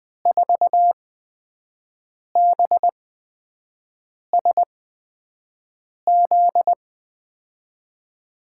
Morse code, heard as a keyed tone, about 20 wpm, spaced as for 5 wpm, 700 Hz. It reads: 4BSZ
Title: USA Call Signs - 6 Characters - 8x Character Spacing - 20wpm